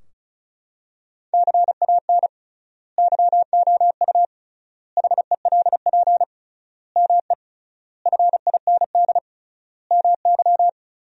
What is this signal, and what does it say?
Morse code, heard as a keyed tone, about 34 wpm, 700 Hz. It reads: CAN YOU HELP ME FIND MY
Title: 3x Word-Spacing - Sentences from Top 200 Words - 35wpm